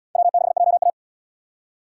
Morse code, heard as a keyed tone, about 71 wpm, 700 Hz. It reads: W6JA